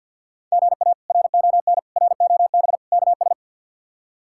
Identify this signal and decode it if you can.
Morse code, heard as a keyed tone, about 47 wpm, 700 Hz. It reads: GA RON ROB KS
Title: K1USN SST QSO Practice - 50wpm